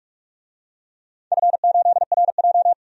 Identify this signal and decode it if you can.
Morse code, heard as a keyed tone, about 46 wpm, 700 Hz. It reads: F8RJ
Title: DXCC-100 Call Signs - 4 Characters - 45wpm